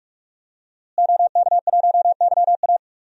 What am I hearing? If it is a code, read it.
Morse code, heard as a keyed tone, about 46 wpm, 700 Hz. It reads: OK1YA
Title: DXCC-100 Call Signs - 5 Characters - Repeated 2x - 45wpm